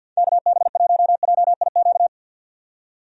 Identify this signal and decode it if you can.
Morse code, heard as a keyed tone, about 50 wpm, 700 Hz. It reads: KB1JIQ
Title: USA Call Signs - 6 Characters - 50wpm